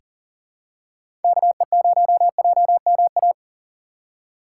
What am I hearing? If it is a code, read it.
Morse code, heard as a keyed tone, about 40 wpm, 700 Hz. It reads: KE0JMA
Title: USA Call Signs - 6 Characters - 2x Character Spacing - 40wpm